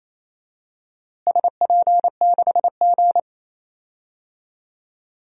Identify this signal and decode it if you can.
Morse code, heard as a keyed tone, about 28 wpm, 700 Hz. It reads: SP6G
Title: DXCC-100 Call Signs - 4 Characters - Repeated 2x - 28wpm